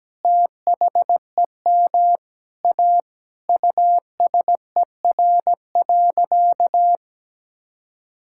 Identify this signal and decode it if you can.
Morse code, heard as a keyed tone, about 17 wpm, 700 Hz. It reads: THEM A USER.
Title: Sentences from Top 1000 Words - Review Part 5 - 17wpm